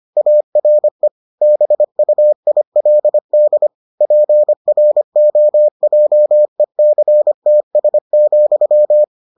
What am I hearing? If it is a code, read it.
Morse code, heard as a keyed tone, about 25 wpm, 600 Hz. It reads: ARE BUILD PROJECTS,